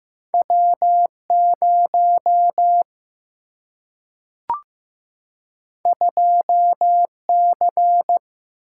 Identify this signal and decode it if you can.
Morse code, heard as a keyed tone, about 15 wpm, 700 Hz. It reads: W0 2C
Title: Single-Letter Single-Number 15wpm